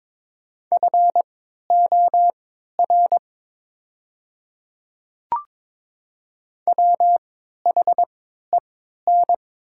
Morse code, heard as a keyed tone, about 22 wpm, spaced as for 12 wpm, 700 Hz. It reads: FOR WHEN
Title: 3x Character Spacing - Top 100 Words - 22wpm